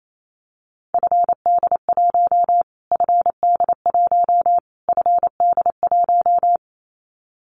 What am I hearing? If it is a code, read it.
Morse code, heard as a keyed tone, about 28 wpm, 700 Hz. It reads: FD1 FD1 FD1
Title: Callsign Prefixes for Top 25 DXCC - Repeated 3x - 28wpm